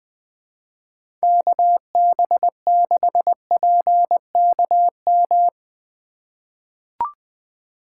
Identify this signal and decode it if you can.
Morse code, heard as a keyed tone, about 20 wpm, 700 Hz. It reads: KB6PKM
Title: USA Call Signs - 6 Characters - 3x Character Spacing - 20wpm